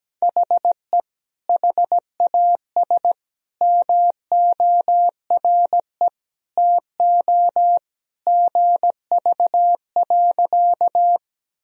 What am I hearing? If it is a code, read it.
Morse code, heard as a keyed tone, about 17 wpm, 700 Hz. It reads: HE HAS MORE TO GV.
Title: Sentences from Top 200 Words and Abbreviations 17wpm